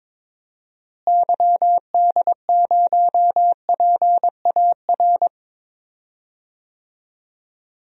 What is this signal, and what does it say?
Morse code, heard as a keyed tone, about 21 wpm, 700 Hz. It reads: YD0PAR
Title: DXCC-100 Call Signs - 6 Characters - Repeated 2x - 22wpm